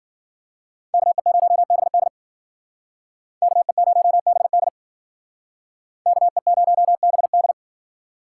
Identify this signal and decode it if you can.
Morse code, heard as a keyed tone, about 51 wpm, 700 Hz. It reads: KE0BD KE0BD KE0BD